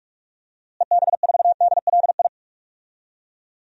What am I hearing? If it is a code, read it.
Morse code, heard as a keyed tone, about 41 wpm, 690 Hz. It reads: EB4DLI
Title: DXCC-100 Call Signs - 6 Characters - Repeated 2x - 45wpm